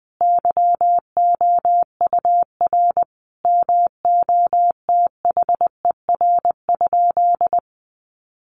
Morse code, heard as a keyed tone, about 20 wpm, 700 Hz. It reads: YOUR MOTHER?
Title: Sentences from Top 200 Words 20wpm